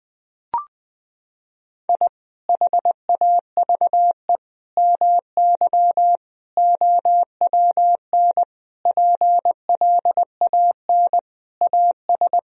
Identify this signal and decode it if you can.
Morse code, heard as a keyed tone, about 21 wpm, 700 Hz. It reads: I HAVE MY OWN PLAN AS
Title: Sentences from Top 1000 Words - Review Part 2 - 20wpm